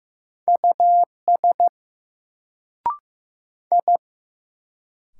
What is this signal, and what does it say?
Morse code, heard as a keyed tone, about 15 wpm, 700 Hz. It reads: US I
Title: Sentences from Top 200 Words - New Word Drill - 15wpm